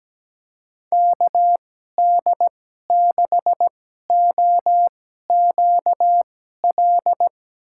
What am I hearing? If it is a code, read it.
Morse code, heard as a keyed tone, about 17 wpm, spaced as for 12 wpm, 700 Hz. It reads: KD6OQL
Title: USA Call Signs - 6 Characters - 2x Character Spacing - 17wpm